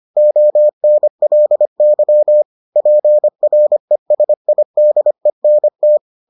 Morse code, heard as a keyed tone, about 25 wpm, 600 Hz. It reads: ONLY PRESIDENT